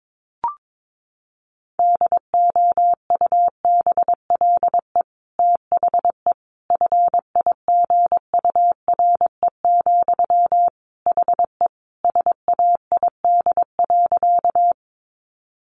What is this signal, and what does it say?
Morse code, heard as a keyed tone, about 22 wpm, 700 Hz. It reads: DOUBLE THE FIGURE, HE SAID.
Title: Sentences from Top 1400 Words 22wpm